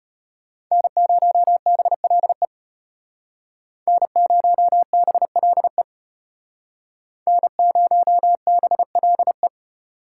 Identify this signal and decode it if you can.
Morse code, heard as a keyed tone, about 33 wpm, 700 Hz. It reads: N0BLE N0BLE N0BLE